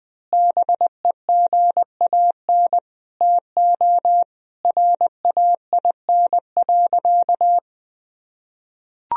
Morse code, heard as a keyed tone, about 21 wpm, 700 Hz. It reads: BEGAN TO RAIN.
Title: Sentences from Top 1000 Words - Review Part 1 - 20wpm